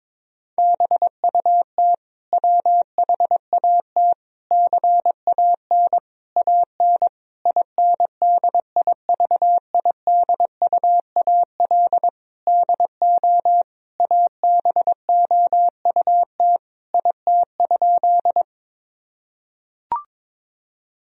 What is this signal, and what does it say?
Morse code, heard as a keyed tone, about 21 wpm, 700 Hz. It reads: BUT WHAT CAN AN INDIVIDUAL DO ABOUT IT?
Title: Sentences from Top 1000 Words - Review Part 4 - 22wpm